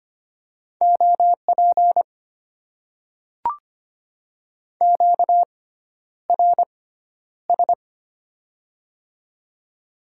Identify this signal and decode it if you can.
Morse code, heard as a keyed tone, about 25 wpm, spaced as for 11 wpm, 690 Hz. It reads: OP QRS
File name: Top 90 QSO Elements - 6x Character Spacing - 25wpm